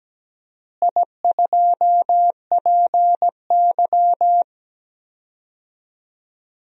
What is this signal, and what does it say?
Morse code, heard as a keyed tone, about 17 wpm, 700 Hz. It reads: I2PY
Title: DXCC-100 Call Signs - 4 Characters - Repeated 3x - 17wpm